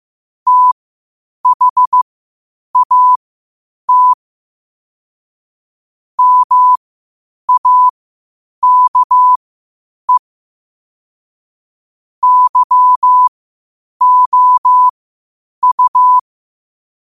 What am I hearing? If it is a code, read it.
Morse code, heard as a keyed tone, about 15 wpm, spaced as for 8 wpm, 1000 Hz. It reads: THAT MAKE YOU
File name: 8wpm